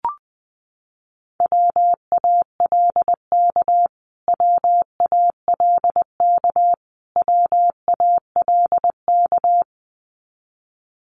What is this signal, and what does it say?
Morse code, heard as a keyed tone, about 20 wpm, 700 Hz. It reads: WALK WALK WALK
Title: Top 500 Words Repeated 20wpm